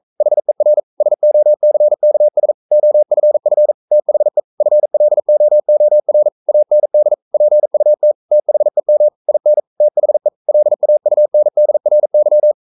Morse code, heard as a keyed tone, about 42 wpm, 600 Hz. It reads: HER SOCKS OFF THE FLOOR AND PUT THEM IN THE LAUNDRY
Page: Romanen Never af Ken Follet på engelsk.